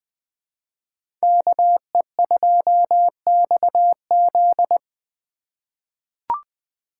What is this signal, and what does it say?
Morse code, heard as a keyed tone, about 20 wpm, 700 Hz. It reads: KE2XZ
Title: USA Call Signs - 5 Characters - 4x Character Spacing - 20wpm